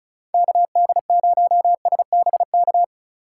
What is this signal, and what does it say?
Morse code, heard as a keyed tone, about 34 wpm, 700 Hz. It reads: KD0SBK
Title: USA Call Signs - 6 Characters - 4x Character Spacing - 35wpm